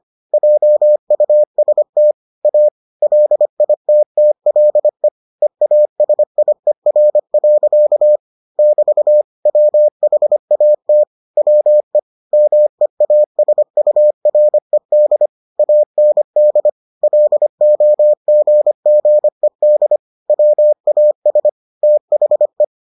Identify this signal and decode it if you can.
Morse code, heard as a keyed tone, about 25 wpm, 600 Hz. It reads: JUST A LITTLE EASIER. <BT> WHAT WE MEASURED AND LOGGED WAS THE